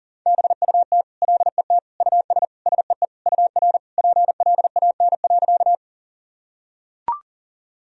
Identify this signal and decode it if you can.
Morse code, heard as a keyed tone, about 40 wpm, 700 Hz. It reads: BUT LET US SEE UR PLAN.
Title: Sentences from Top 500 Words and Abbreviations 40wpm